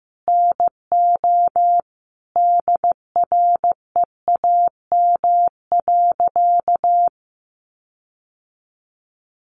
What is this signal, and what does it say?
Morse code, heard as a keyed tone, about 15 wpm, 700 Hz. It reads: NO DREAM.
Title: Sentences from Top 1500 Words 15wpm